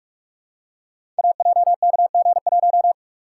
Morse code, heard as a keyed tone, about 44 wpm, 700 Hz. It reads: AJKG1
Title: Code Groups - 5 Letter:Numbers - 45wpm